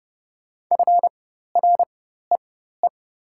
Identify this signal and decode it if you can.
Morse code, heard as a keyed tone, about 30 wpm, spaced as for 14 wpm, 690 Hz. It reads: FREE
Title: 4x Character Spacing - Top 500 Words - 30wpm